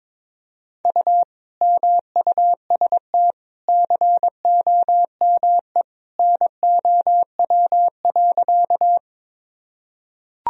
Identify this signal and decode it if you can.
Morse code, heard as a keyed tone, about 22 wpm, 700 Hz. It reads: U MUST COME NOW.
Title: Sentences from Top 200 Words and Abbreviations 22wpm